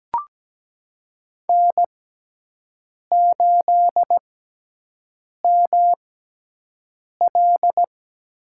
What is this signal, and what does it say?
Morse code, heard as a keyed tone, about 17 wpm, spaced as for 6 wpm, 700 Hz. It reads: N8ML
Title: USA Call Signs - 5 Characters - 6x Character Spacing - 17wpm